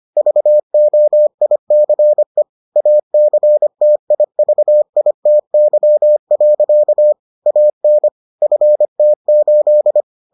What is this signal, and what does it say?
Morse code, heard as a keyed tone, about 25 wpm, 600 Hz. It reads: VOICE ACTIVITY. AN FT8